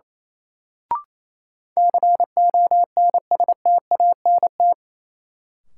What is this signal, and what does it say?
Morse code, heard as a keyed tone, about 28 wpm, 700 Hz. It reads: CONSTANT
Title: Sentences from Top 1500 Words - New Word Drill - 28wpm